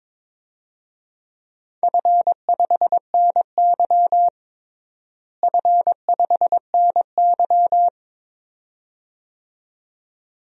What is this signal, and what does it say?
Morse code, heard as a keyed tone, about 22 wpm, 700 Hz. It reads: F5NY F5NY
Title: DXCC-100 Call Signs - 4 Characters - Repeated 2x - 22wpm